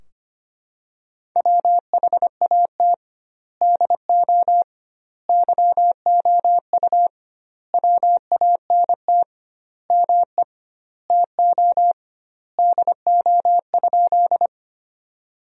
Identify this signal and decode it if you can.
Morse code, heard as a keyed tone, about 25 wpm, 700 Hz. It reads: WHAT DO YOU WANT ME TO DO?
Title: 2x Word-Spacing - Sentences from Top 200 Words - 25wpm